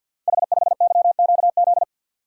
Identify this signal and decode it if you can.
Morse code, heard as a keyed tone, about 50 wpm, 700 Hz. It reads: H5YQZ
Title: Code Groups - 5 Letter:Numbers - 50wpm